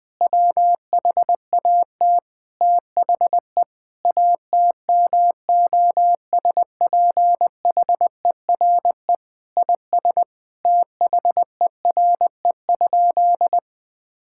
Morse code, heard as a keyed tone, about 21 wpm, 700 Hz. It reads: WHAT THE ATMOSPHERE IS THERE?
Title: Sentences from Top 1600 Words 20wpm